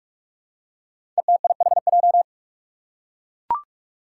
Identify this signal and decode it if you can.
Morse code, heard as a keyed tone, about 44 wpm, 700 Hz. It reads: ETIHJ
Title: Five Challenging Letter-Number Combinations 45wpm